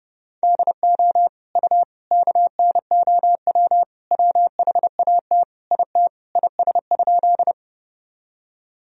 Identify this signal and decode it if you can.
Morse code, heard as a keyed tone, about 30 wpm, 700 Hz. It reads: DO U KNOW WHAT IT IS?
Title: Sentences from Top 100 Words and Abbreviations 30wpm